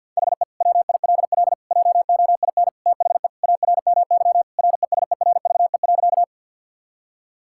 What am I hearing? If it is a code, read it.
Morse code, heard as a keyed tone, about 52 wpm, 700 Hz. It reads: HE WILL JOIN THE ARMY RESERVE.
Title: Sentences from Top 1000 Words - Review Part 6 - 50wpm